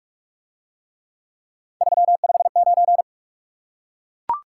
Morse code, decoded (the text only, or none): 359